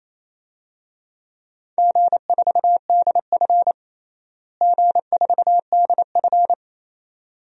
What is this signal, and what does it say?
Morse code, heard as a keyed tone, about 28 wpm, 700 Hz. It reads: G4DF G4DF
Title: DXCC-100 Call Signs - 4 Characters - Repeated 2x - 28wpm